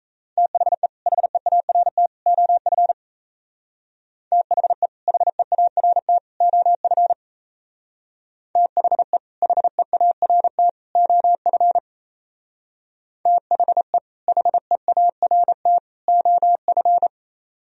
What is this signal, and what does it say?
Morse code, heard as a keyed tone, about 32 wpm, 700 Hz. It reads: THE HEART OF THE HEART OF THE HEART OF THE HEART OF
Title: Speed-Racing - Sets of 3 Words 28wpm